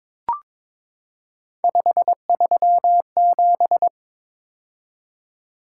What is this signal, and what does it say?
Morse code, heard as a keyed tone, about 21 wpm, 700 Hz. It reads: 537
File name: practice / 22wpm / Three Letter-Number Combinations - 22wpm